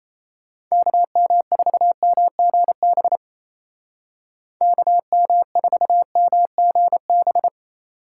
Morse code, heard as a keyed tone, about 36 wpm, 700 Hz. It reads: KM4MGB KM4MGB